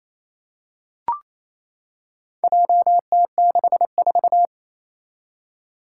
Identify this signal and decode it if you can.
Morse code, heard as a keyed tone, about 28 wpm, 700 Hz. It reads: JT64
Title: Four Challenging Letter-Number Combinations 28wpm